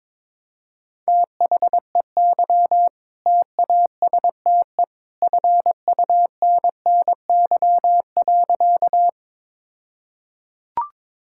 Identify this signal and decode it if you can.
Morse code, heard as a keyed tone, about 22 wpm, 700 Hz. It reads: THEY TASTE FUNNY.
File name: All-Age Appropriate Jokes - 22wpm